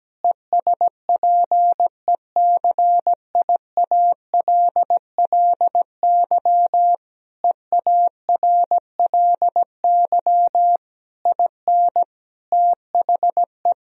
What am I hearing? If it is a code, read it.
Morse code, heard as a keyed tone, about 17 wpm, 700 Hz. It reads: ESPECIALLY EARLY IN THE
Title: Sentences from Top 1000 Words - Review Part 4 - 17wpm